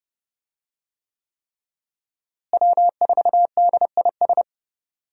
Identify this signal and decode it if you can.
Morse code, heard as a keyed tone, about 32 wpm, 700 Hz. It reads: W4DIS
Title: USA Call Signs - 5 Characters - Repeated 3x - 30wpm